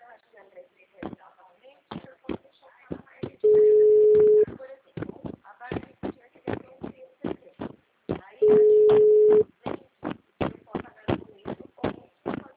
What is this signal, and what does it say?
Ringback tone (Europe, 425 Hz), with speech.